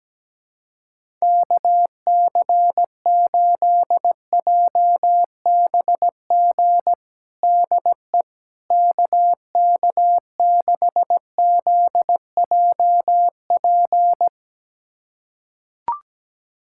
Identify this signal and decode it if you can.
Morse code, heard as a keyed tone, about 17 wpm, 700 Hz. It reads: KC8JBG DE KK6ZJP
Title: USA Callsign DE Callsign - 6 Characters - 17wpm